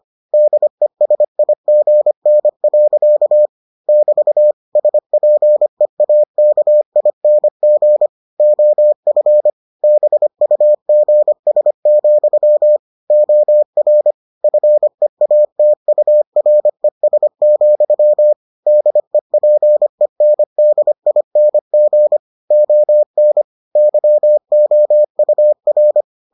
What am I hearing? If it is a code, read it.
Morse code, heard as a keyed tone, about 25 wpm, 600 Hz. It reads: DESIGN. <BT> SPEAKING OF BUGS, OR FEATURES, DEPENDING ON YOUR